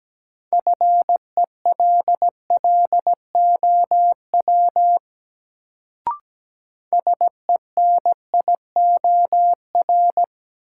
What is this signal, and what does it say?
Morse code, heard as a keyed tone, about 17 wpm, 700 Hz. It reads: FELLOW SENIOR
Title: Sets of 5 Words Encore - New Word Drill - 17wpm